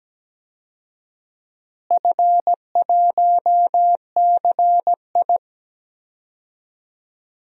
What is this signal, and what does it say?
Morse code, heard as a keyed tone, about 17 wpm, 700 Hz. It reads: F1CI
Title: DXCC-100 Call Signs - 4 Characters - Repeated 3x - 17wpm